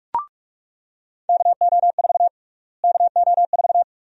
Morse code, heard as a keyed tone, about 41 wpm, 700 Hz. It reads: KO4 KO4
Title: USA Call Sign Prefixes - Repeated 2x - 45wpm